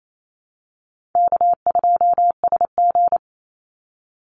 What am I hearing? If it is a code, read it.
Morse code, heard as a keyed tone, about 28 wpm, 700 Hz. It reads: K2SG
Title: Easy Call Signs - 28wpm